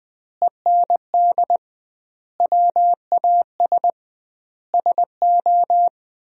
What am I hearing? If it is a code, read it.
Morse code, heard as a keyed tone, about 20 wpm, 700 Hz. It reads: END WAS SO
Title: 2x Word-Spacing - Sentences from Top 300 Words - 20wpm